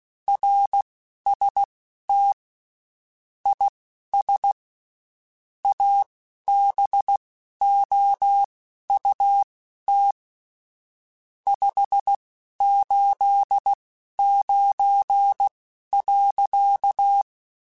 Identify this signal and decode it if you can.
Morse code, heard as a keyed tone, about 16 wpm, spaced as for 11 wpm, 800 Hz. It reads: RST IS ABOUT 589.
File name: QSOs / QSOs-mp3s / WD0FBT 1@15